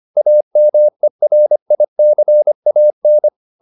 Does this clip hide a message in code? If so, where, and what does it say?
Yes: Morse code from 0.2 to 3.3 s, heard as a keyed tone, about 25 wpm, 600 Hz. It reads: AMERICAN